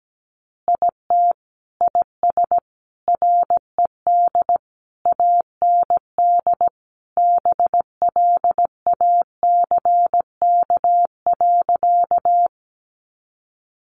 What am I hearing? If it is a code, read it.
Morse code, heard as a keyed tone, about 17 wpm, 700 Hz. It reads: IT IS RED AND BLACK.
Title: Additional Easy Phrases 17wpm